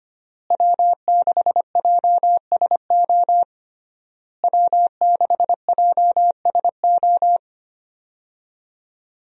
Morse code, heard as a keyed tone, about 25 wpm, 700 Hz. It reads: W6JSO W6JSO
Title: USA Call Signs - 5 Characters - Repeated 3x - 25wpm